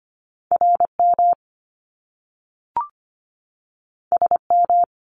Morse code, heard as a keyed tone, about 25 wpm, 700 Hz. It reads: RM SM
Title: Top Two Letter Combinations in Words 25wpm